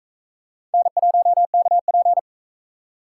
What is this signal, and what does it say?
Morse code, heard as a keyed tone, about 42 wpm, 700 Hz. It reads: N1KP